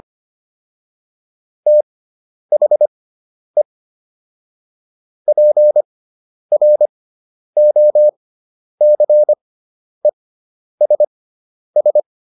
Morse code, heard as a keyed tone, about 25 wpm, spaced as for 10 wpm, 600 Hz. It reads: THE PROCESS